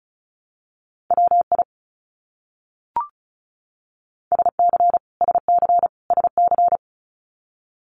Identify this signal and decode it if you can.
Morse code, heard as a keyed tone, about 34 wpm, 700 Hz. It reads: WI SC SC SC
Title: US States Repeated 3x 35wpm